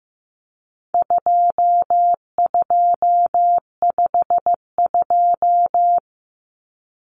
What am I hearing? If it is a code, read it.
Morse code, heard as a keyed tone, about 15 wpm, 700 Hz. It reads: 2252
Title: Common Cut Numbers - 4-Digit 15wpm